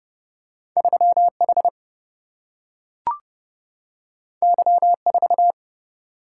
Morse code, heard as a keyed tone, about 30 wpm, 700 Hz. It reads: 3H Y4
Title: Single-Letter Single-Number 30wpm